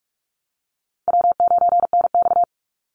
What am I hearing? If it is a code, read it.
Morse code, heard as a keyed tone, about 44 wpm, 700 Hz. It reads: W9NX
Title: Easy Call Signs Repeated 3x - 45wpm